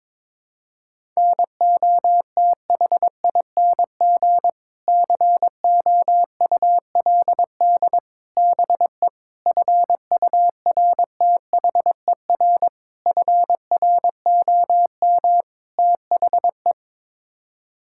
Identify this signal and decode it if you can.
Morse code, heard as a keyed tone, about 22 wpm, 700 Hz. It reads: NOTHING COULD BE FURTHER FROM THE
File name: practice / 22wpm / Sets of 6 Words - 22wpm